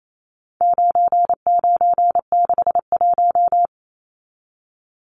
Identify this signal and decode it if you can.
Morse code, heard as a keyed tone, about 28 wpm, 700 Hz. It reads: 9961
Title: Numbers - 4-Digit 28wpm